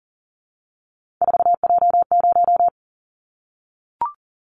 Morse code, heard as a keyed tone, about 41 wpm, 700 Hz. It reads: 4J0
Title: Callsign Prefixes for Top 100 DXCC - 40wpm